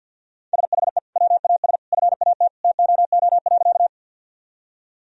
Morse code, heard as a keyed tone, about 50 wpm, 700 Hz. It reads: SHE WAS FAT TOO.
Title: Sentences from Top 1000 Words - Review Part 4 - 50wpm